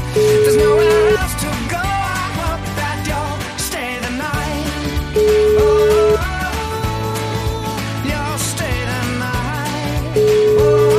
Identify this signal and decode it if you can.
Ringback tone (Europe, 425 Hz), with noise.